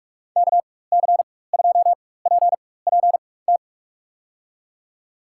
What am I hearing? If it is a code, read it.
Morse code, heard as a keyed tone, about 46 wpm, spaced as for 21 wpm, 700 Hz. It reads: KC2PPT
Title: USA Call Signs - 6 Characters - 4x Character Spacing - 45wpm